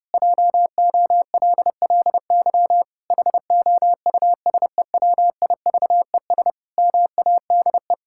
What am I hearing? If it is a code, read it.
Morse code, heard as a keyed tone, about 30 wpm, 700 Hz. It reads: JOLLY HOUSEWIVES MADE
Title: Pangram Challenge v2 - 30wpm